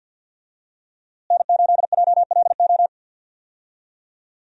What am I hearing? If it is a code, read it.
Morse code, heard as a keyed tone, about 51 wpm, 700 Hz. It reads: N8JLO
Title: USA Call Signs - 5 Characters - Repeated 3x - 50wpm